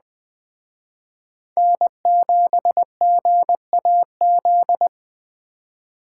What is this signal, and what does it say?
Morse code, heard as a keyed tone, about 20 wpm, 700 Hz. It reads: N7GAZ
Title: Speed Racing - USA Call Signs 17wpm